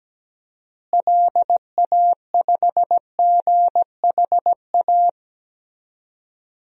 Morse code, heard as a keyed tone, about 17 wpm, 700 Hz. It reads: LA5GHA
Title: DXCC-100 Call Signs - 6 Characters - Repeated 3x - 17wpm